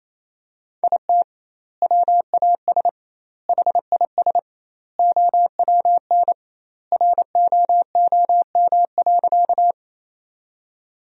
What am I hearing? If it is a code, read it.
Morse code, heard as a keyed tone, about 28 wpm, 700 Hz. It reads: IT WAS HIS OWN ROOM.
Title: 2x Word-Spacing - Sentences from Top 300 Words - 28wpm